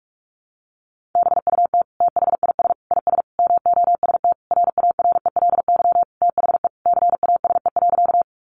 Morse code, heard as a keyed tone, about 44 wpm, 700 Hz. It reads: BUT THIS IS MOST RARELY THE CASE.
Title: Sentences from Top 1400 Words 45wpm